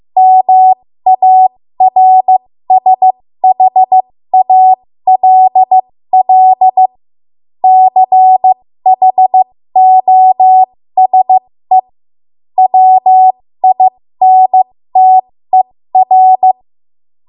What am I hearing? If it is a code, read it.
Morse code, heard as a keyed tone, about 15 wpm, spaced as for 13 wpm, 750 Hz. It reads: MARSHALL CHOSE WINTER
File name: Morse / 13WPM